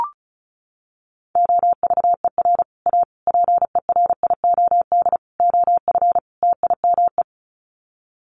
Morse code, heard as a keyed tone, about 34 wpm, 700 Hz. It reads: OVER A PERIOD OF TIME
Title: Sets of 5 Words Encore2 35wpm